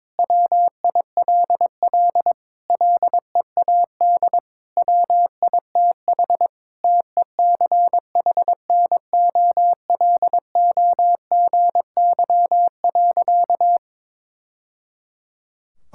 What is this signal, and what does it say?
Morse code, heard as a keyed tone, about 22 wpm, 700 Hz. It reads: WILL LEAD WITH TECHNOLOGY.
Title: Sentences from Top 1000 Words - Review Part 5 - 22wpm